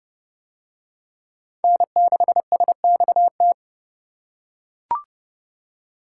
Morse code, heard as a keyed tone, about 28 wpm, 700 Hz. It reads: N6SXT